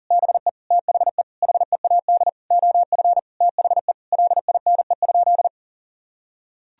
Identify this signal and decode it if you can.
Morse code, heard as a keyed tone, about 37 wpm, 700 Hz. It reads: BE THE HEAD OF THE LINE?
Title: Sentences from Top 1000 Words - Review Part 1 - 40wpm